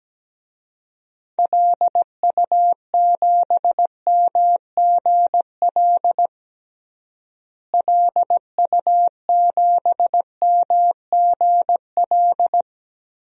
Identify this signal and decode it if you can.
Morse code, heard as a keyed tone, about 17 wpm, 700 Hz. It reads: LU7MGL LU7MGL
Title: DXCC-100 Call Signs - 6 Characters - Repeated 2x - 17wpm